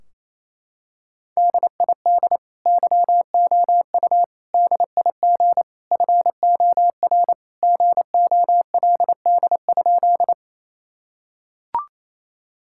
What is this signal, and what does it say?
Morse code, heard as a keyed tone, about 28 wpm, 700 Hz. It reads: DID YOU DIG FOR GOLD?
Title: Sentences from Top 1600 Words 28wpm